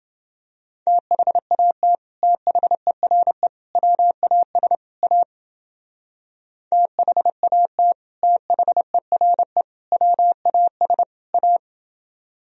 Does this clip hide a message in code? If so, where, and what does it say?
Yes: Morse code from 0.9 to 11.6 s, heard as a keyed tone, about 29 wpm, 700 Hz. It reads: THAT THERE WAS A THAT THERE WAS A